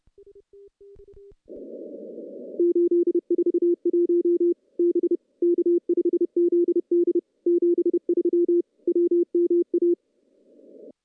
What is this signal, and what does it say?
Morse code, heard as a keyed tone, about 31 wpm, 350 Hz. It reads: 841 B K5ZD 73 WMA
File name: contests / CW-SS-k5tr- -0700